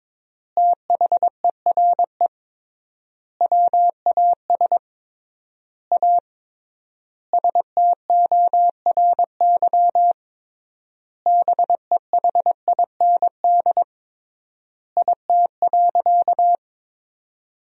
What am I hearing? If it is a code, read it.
Morse code, heard as a keyed tone, about 22 wpm, 700 Hz. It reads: THERE WAS A STORY BEHIND IT.
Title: 3x Word-Spacing - Sentences from Top 500 Words - 22wpm